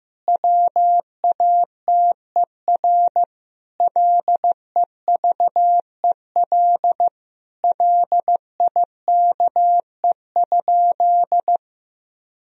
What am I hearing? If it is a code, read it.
Morse code, heard as a keyed tone, about 14 wpm, 700 Hz. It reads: WATER LEVEL LIKE?
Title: Sentences from Top 1000 Words - Review Part 4 - 15wpm